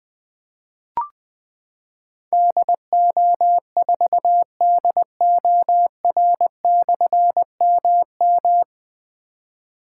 Morse code, heard as a keyed tone, about 20 wpm, 700 Hz. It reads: DO4DOR/MM
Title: Difficult Call Signs 20wpm